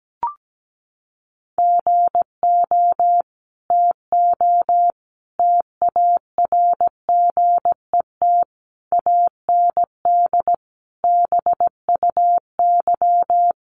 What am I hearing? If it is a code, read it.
Morse code, heard as a keyed tone, about 17 wpm, 700 Hz. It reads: GO TO TARGET AND BUY
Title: Sentences from Top 1200 Words 17wpm